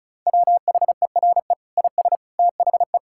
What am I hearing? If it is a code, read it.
Morse code, heard as a keyed tone, about 34 wpm, 700 Hz. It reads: WHERE IS THE
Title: Sentences from Top 1600 Words 35wpm